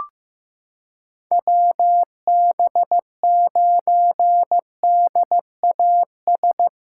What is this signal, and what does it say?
Morse code, heard as a keyed tone, about 15 wpm, 700 Hz. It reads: WB9DAS
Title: USA Callsign DE Callsign - 6 Characters - 15wpm